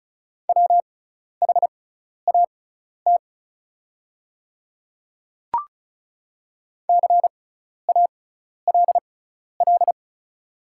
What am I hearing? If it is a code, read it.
Morse code, heard as a keyed tone, about 36 wpm, spaced as for 12 wpm, 690 Hz. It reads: WHAT CALL
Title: 6x Character Spacing - Top 100 Words - 35wpm